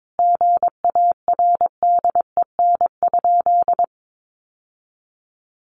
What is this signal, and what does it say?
Morse code, heard as a keyed tone, about 22 wpm, 700 Hz. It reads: GARDEN?
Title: Sentences from Top 700 Words 22wpm